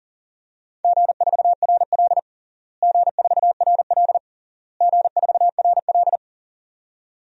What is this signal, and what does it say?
Morse code, heard as a keyed tone, about 40 wpm, 700 Hz. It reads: G4RL G4RL G4RL
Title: DXCC-100 Call Signs - 4 Characters - Repeated 3x - 40wpm